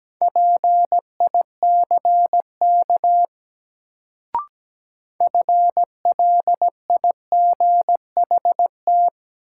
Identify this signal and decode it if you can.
Morse code, heard as a keyed tone, about 17 wpm, 700 Hz. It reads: PICK FLIGHT
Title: Sentences from Top 1000 Words - New Word Drill - 17wpm